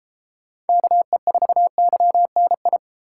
Morse code, heard as a keyed tone, about 31 wpm, 700 Hz. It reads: KE4YNI